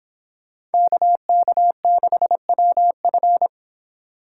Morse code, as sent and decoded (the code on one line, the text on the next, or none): -.- -.- -.... .-- ..-.
KK6WF